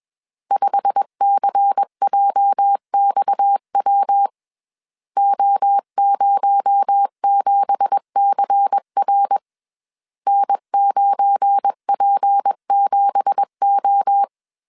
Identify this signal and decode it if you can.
Morse code, heard as a keyed tone, about 21 wpm, 800 Hz. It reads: 5CJXW O07CR N9P7O
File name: CW / DET / 1 9 1500C S C 20wpm 2310